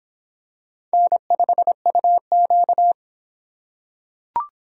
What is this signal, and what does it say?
Morse code, heard as a keyed tone, about 23 wpm, 700 Hz. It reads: N5UQ